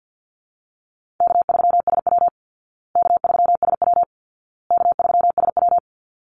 Morse code, heard as a keyed tone, about 49 wpm, 700 Hz. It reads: K3SW K3SW K3SW
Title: Easy Call Signs Repeated 3x - 50wpm